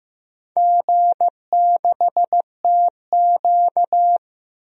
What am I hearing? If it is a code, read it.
Morse code, heard as a keyed tone, about 15 wpm, 700 Hz. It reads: G6TQ
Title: DXCC-100 Call Signs - 4 Characters - 15wpm